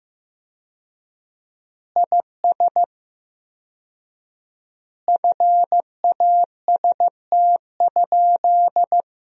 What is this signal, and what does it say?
Morse code, heard as a keyed tone, about 15 wpm, 700 Hz. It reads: IS FAST?
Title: 4x Word-Spacing - Sentences from Top 400 Words - 15wpm